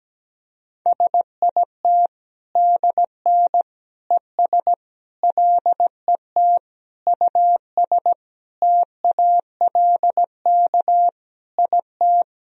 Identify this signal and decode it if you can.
Morse code, heard as a keyed tone, about 17 wpm, 700 Hz. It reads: SIT DN ES LET US TALK IT
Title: Sentences from Top 500 Words and Abbreviations 17wpm